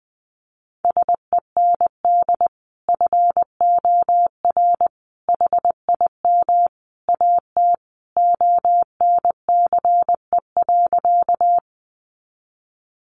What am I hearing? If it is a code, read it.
Morse code, heard as a keyed tone, about 20 wpm, 700 Hz. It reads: SEND FOR HIM AT ONCE.
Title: Sentences from Top 1500 Words 20wpm